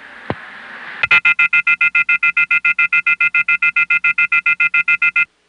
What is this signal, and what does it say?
Off-hook warning tone, 1400 + 2060 + 2450 + 2600 Hz.